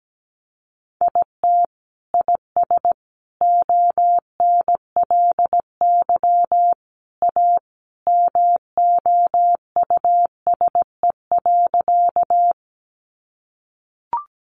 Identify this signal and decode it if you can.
Morse code, heard as a keyed tone, about 17 wpm, 700 Hz. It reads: IT IS ONLY A MOUSE.
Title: Sentences from Top 700 Words 17wpm